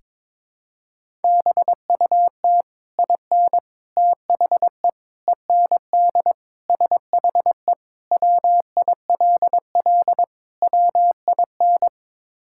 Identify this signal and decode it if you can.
Morse code, heard as a keyed tone, about 22 wpm, 700 Hz. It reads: BUT IN THE END SHE WILL WIN
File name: practice / 22wpm / Sentences with 3-Word Phrases - 22wpm